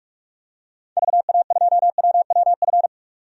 Morse code, heard as a keyed tone, about 46 wpm, 700 Hz. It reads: VA2WWF
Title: DXCC-100 Call Signs - 6 Characters - 45wpm